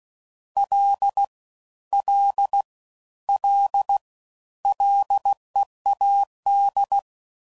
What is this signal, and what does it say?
Morse code, heard as a keyed tone, about 16 wpm, 800 Hz. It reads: L L L LEAD